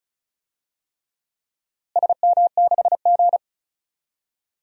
Morse code, heard as a keyed tone, about 36 wpm, 700 Hz. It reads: SM6G
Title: DXCC-100 Call Signs - 4 Characters - Repeated 3x - 35wpm